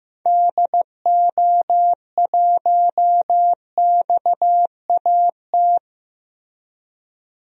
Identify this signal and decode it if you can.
Morse code, heard as a keyed tone, about 15 wpm, 700 Hz. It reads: DO1XAT
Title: DXCC-100 Call Signs - 6 Characters - Repeated 2x - 15wpm